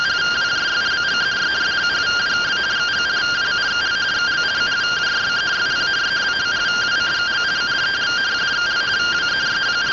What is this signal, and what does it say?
Two-tone teleprinter signal, about 170 Hz shift.
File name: findings / 81 axial iodine 262.gen-45 rtty rx